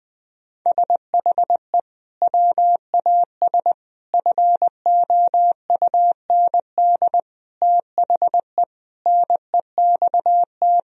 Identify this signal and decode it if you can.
Morse code, heard as a keyed tone, about 20 wpm, 700 Hz. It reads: SHE WAS FOUND THE NEXT
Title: Sentences from Top 1000 Words - Review Part 3 - 20wpm